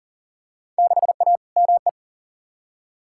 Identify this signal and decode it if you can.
Morse code, heard as a keyed tone, about 39 wpm, 700 Hz. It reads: 6A ME
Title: Field Day Exchanges - 40wpm